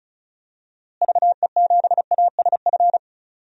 Morse code, heard as a keyed tone, about 36 wpm, 700 Hz. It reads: VE7ASF
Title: DXCC-100 Call Signs - 6 Characters - Repeated 3x - 35wpm